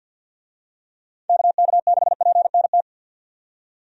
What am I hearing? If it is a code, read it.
Morse code, heard as a keyed tone, about 51 wpm, 700 Hz. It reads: KK6PNT
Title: USA Call Signs - 6 Characters - 50wpm